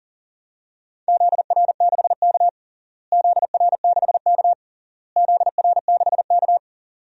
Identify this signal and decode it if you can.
Morse code, heard as a keyed tone, about 40 wpm, 700 Hz. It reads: ZR6K ZR6K ZR6K
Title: DXCC-100 Call Signs - 4 Characters - Repeated 3x - 40wpm